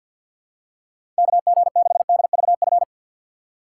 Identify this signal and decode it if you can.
Morse code, heard as a keyed tone, about 48 wpm, 700 Hz. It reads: KG6DVF
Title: USA Call Signs - 6 Characters - 4x Character Spacing - 50wpm